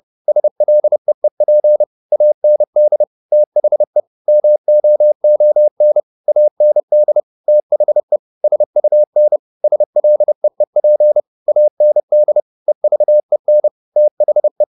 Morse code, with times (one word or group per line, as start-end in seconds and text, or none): 0.3-1.8 SLEEP
2.1-3.0 AND
3.3-4.0 THE
4.3-6.0 MOON
6.3-7.2 AND
7.5-8.2 THE
8.4-9.4 SUN
9.6-11.2 SLEEP
11.5-12.4 AND
12.7-13.7 EVEN
14.0-14.6 THE